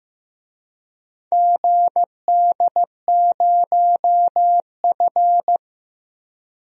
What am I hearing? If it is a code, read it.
Morse code, heard as a keyed tone, about 15 wpm, 700 Hz. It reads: GD0F
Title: DXCC-100 Call Signs - 4 Characters - 15wpm